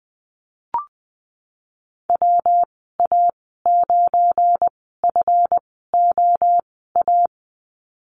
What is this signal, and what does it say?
Morse code, heard as a keyed tone, about 20 wpm, spaced as for 15 wpm, 700 Hz. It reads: WA9FOA
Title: 193 Lesson 31g - Letter v - Call Signs 20wpm